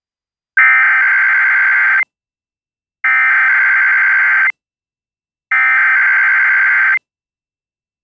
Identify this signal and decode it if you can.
Emergency Alert System (SAME) tones (header bursts), heard as harsh buzzing bursts. It is heard from 0.6 to 6.9 s.